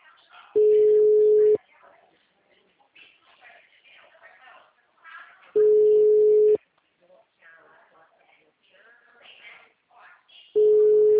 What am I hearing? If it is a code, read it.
Ringback tone (Europe, 425 Hz), with speech.